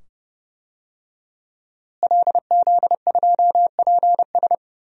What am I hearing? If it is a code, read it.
Morse code, heard as a keyed tone, about 30 wpm, 700 Hz. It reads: LZ2PS
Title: DXCC-100 Call Signs - 5 Characters - Repeated 2x - 30wpm